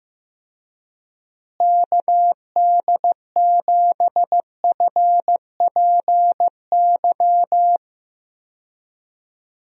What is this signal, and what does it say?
Morse code, heard as a keyed tone, about 15 wpm, 700 Hz. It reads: KD7FPY
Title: USA Call Signs - 6 Characters - Repeated 3x - 15wpm